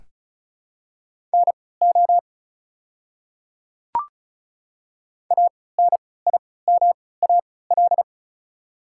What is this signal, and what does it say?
Morse code, heard as a keyed tone, about 36 wpm, spaced as for 20 wpm, 700 Hz. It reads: NO ANIMAL
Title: 3x Character Spacing - Top 200 Words - 35wpm